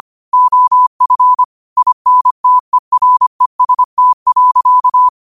Morse code, heard as a keyed tone, about 25 wpm, 1000 Hz. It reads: OF INTEREST.
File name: qotdfiles / 25wpm